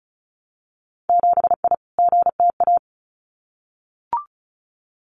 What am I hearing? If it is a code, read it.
Morse code, heard as a keyed tone, about 36 wpm, 700 Hz. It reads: 7I GTA
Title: Winter Field Day Exchanges Encore 35wpm